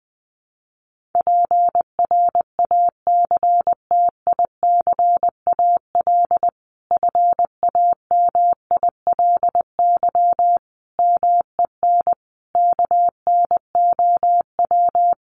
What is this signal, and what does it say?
Morse code, heard as a keyed tone, about 20 wpm, 700 Hz. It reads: PRACTICAL FAMILY MEN KNOW
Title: Sentences from Top 1400 Words 20wpm